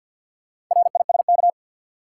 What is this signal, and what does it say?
Morse code, heard as a keyed tone, about 47 wpm, 700 Hz. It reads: RISK
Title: Sentences from Top 600 Words - New Word Drill - 50wpm